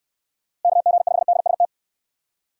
Morse code, heard as a keyed tone, about 62 wpm, 700 Hz. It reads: KG5DST